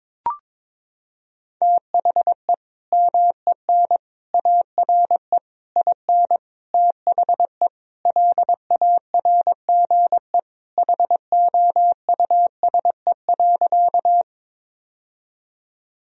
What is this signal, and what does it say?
Morse code, heard as a keyed tone, about 21 wpm, 700 Hz. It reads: THE MEN ARE IN THE LARGE HOUSE.
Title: Sentences from Top 1000 Words - Review Part 1 - 22wpm